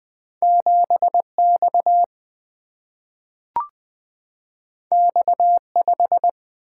Morse code, heard as a keyed tone, about 20 wpm, 700 Hz. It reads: 7X X5
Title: Single-Letter Single-Number 20wpm